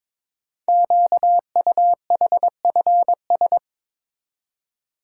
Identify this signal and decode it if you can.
Morse code, heard as a keyed tone, about 22 wpm, 700 Hz. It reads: QUHFS
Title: Code Groups - 5 Letters - 22wpm